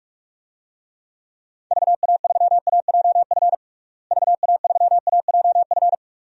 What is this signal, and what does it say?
Morse code, heard as a keyed tone, about 46 wpm, 700 Hz. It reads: VA3AJF VA3AJF
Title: DXCC-100 Call Signs - 6 Characters - Repeated 3x - 45wpm